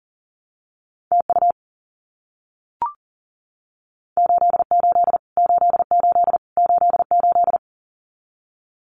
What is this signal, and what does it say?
Morse code, heard as a keyed tone, about 39 wpm, 700 Hz. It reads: TU 88 88 88
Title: Abbreviations Repeated 3x - 40wpm